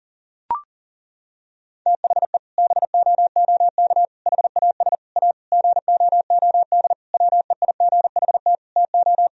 Morse code, heard as a keyed tone, about 38 wpm, 700 Hz. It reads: THE BOOK HAS A GOOD WEIGHT TO
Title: Sentences from Top 1000 Words - Review Part 3 - 40wpm